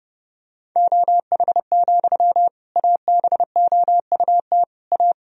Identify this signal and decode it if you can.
Morse code, heard as a keyed tone, about 30 wpm, 700 Hz. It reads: OH, ABOUT A
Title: Sentences from Top 1000 Words - Review Part 2 - 30wpm